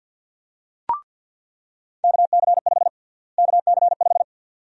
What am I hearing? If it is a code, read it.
Morse code, heard as a keyed tone, about 48 wpm, 700 Hz. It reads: KC5 KC5
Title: USA Call Sign Prefixes - Repeated 2x - 50wpm